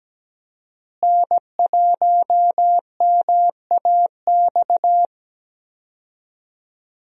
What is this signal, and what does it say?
Morse code, heard as a keyed tone, about 17 wpm, 700 Hz. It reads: N1MAX
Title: USA Call Signs - 5 Characters - Repeated 3x - 17wpm